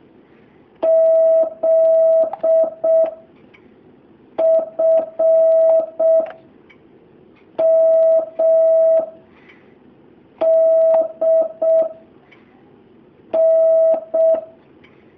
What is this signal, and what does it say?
Morse code, heard as a keyed tone, about 6 wpm, spaced as for 4 wpm, 650 Hz. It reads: ZFMD N
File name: jh8cbh / 2019cw